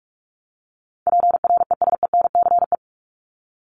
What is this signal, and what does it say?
Morse code, heard as a keyed tone, about 44 wpm, 700 Hz. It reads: PRESENCE
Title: Top 1751-2000 Words 45wpm